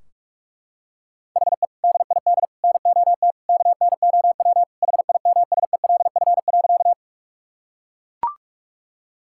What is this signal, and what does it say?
Morse code, heard as a keyed tone, about 42 wpm, 700 Hz. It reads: HE DID NOT KNOW HIMSELF.
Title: Sentences from Top 1600 Words 45wpm